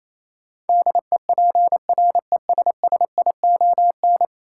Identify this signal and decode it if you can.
Morse code, heard as a keyed tone, about 28 wpm, 700 Hz. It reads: DEPRESSION
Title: Sentences from Top 1300 Words - New Word Drill Alt - 28wpm